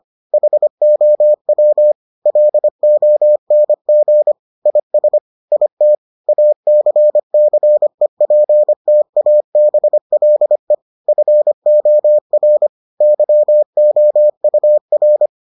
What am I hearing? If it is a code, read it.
Morse code, heard as a keyed tone, about 25 wpm, 600 Hz. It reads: HOW LONG IS IT ACCEPTABLE FOR YOUR